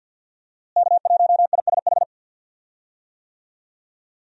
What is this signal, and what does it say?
Morse code, heard as a keyed tone, about 49 wpm, 700 Hz. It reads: K1ISH
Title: USA Call Signs - 5 Characters - Repeated 3x - 50wpm